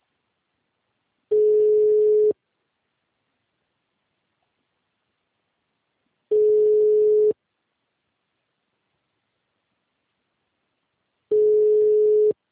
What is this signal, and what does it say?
Ringback tone (Europe, 425 Hz).